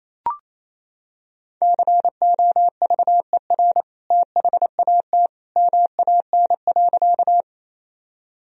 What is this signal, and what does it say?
Morse code, heard as a keyed tone, about 28 wpm, 700 Hz. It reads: COVER THAT MAN.
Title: Sentences from Top 1000 Words - Review Part 1 - 28wpm